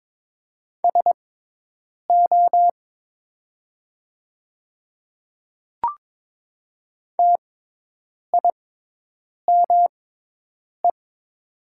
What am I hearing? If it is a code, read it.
Morse code, heard as a keyed tone, about 22 wpm, spaced as for 8 wpm, 700 Hz. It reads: SO TIME
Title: 6x Character Spacing - Top 500 Words - 22wpm